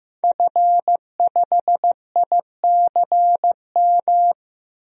Morse code, heard as a keyed tone, about 15 wpm, 700 Hz. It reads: F5ICM
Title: DXCC-100 Call Signs - 5 Characters - 15wpm